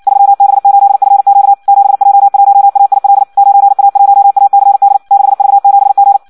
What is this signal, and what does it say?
Morse code, heard as a keyed tone, about 56 wpm, 800 Hz. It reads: 3B8FQ 7J1AIL 9A1ACD 6V7G
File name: Audio practica CW / qrq-tx-58 2